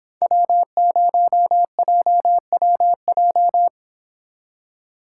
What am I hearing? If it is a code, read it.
Morse code, heard as a keyed tone, about 25 wpm, 700 Hz. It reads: W0JWJ